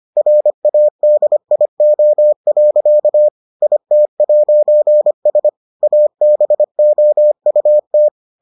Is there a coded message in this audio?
Yes: Morse code from 0.2 to 8.1 s, heard as a keyed tone, about 25 wpm, 600 Hz. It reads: RADIO. IT'S ABOUT